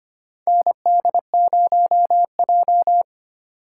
Morse code, heard as a keyed tone, about 25 wpm, 700 Hz. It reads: ND0J
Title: USA Call Signs - 4 Characters - 2x Character Spacing - 25wpm